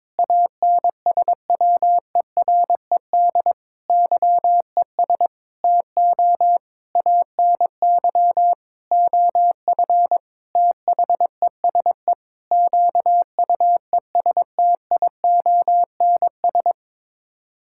Morse code, heard as a keyed tone, about 22 wpm, 700 Hz. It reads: ANSWERED YES TO ANY OF THESE QUESTIONS
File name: Sets of 9 Words - 22wpm